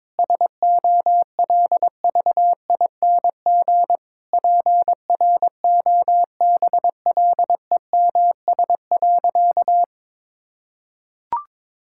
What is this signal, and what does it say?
Morse code, heard as a keyed tone, about 22 wpm, 700 Hz. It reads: SOLVING PROBLEMS.
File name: Difficult Sentences with Hints - 22wpm